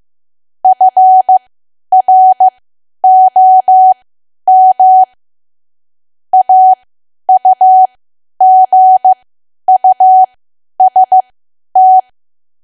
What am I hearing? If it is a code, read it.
Morse code, heard as a keyed tone, about 15 wpm, spaced as for 10 wpm, 750 Hz. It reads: FROM AUGUST
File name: Archive / 10 WPM / 10WPM